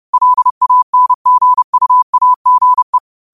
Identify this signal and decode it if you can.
Morse code, heard as a keyed tone, about 30 wpm, 1000 Hz. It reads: LANGUAGE